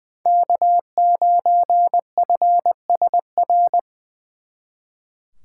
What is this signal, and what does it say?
Morse code, heard as a keyed tone, about 23 wpm, 700 Hz. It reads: K9FSR
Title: Speed Racing - USA 1x3 Call Signs 20wpm